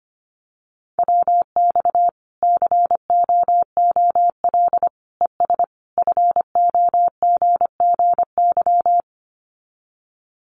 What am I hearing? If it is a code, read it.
Morse code, heard as a keyed tone, about 25 wpm, 700 Hz. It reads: WX COOL ES FOGGY
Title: Common QSO Phrases 25wpm